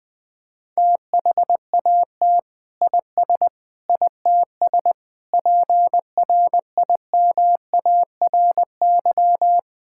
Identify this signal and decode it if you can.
Morse code, heard as a keyed tone, about 20 wpm, 700 Hz. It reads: THAT IS ITS PRIMARY
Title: Sentences from Top 1000 Words - Review Part 5 - 20wpm